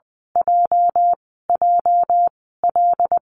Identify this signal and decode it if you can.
Morse code, heard as a keyed tone, about 19 wpm, spaced as for 14 wpm, 700 Hz. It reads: JJL
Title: 207 Lesson 33g - Letter j - Call Signs 20wpm